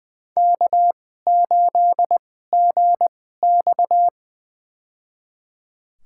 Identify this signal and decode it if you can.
Morse code, heard as a keyed tone, about 21 wpm, spaced as for 15 wpm, 700 Hz. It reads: K8GX
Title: USA Call Signs - 4 Characters - 2x Character Spacing - 20wpm